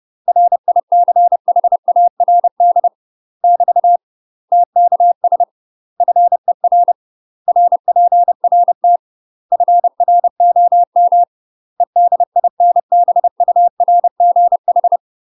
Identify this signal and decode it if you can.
Morse code, heard as a keyed tone, about 30 wpm, 700 Hz. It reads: RICHARD <BT> TKS FER RPRT FROM EDINBURGH
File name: UK MASTER QSOs TRAINING/30WPM